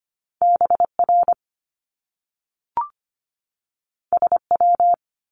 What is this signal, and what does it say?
Morse code, heard as a keyed tone, about 25 wpm, 700 Hz. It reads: BR SW
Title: Top Two Letter Combinations in Words 25wpm